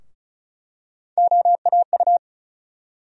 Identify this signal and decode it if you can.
Morse code, heard as a keyed tone, about 34 wpm, 700 Hz. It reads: OAU
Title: USA Call Sign Postfixes - 3 Letters 35wpm